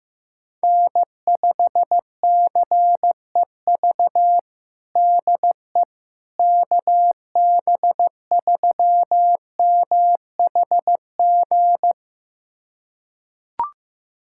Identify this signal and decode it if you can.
Morse code, heard as a keyed tone, about 14 wpm, 700 Hz. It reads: N5CEV DE KB3MHG
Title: USA Callsign DE Callsign 15wpm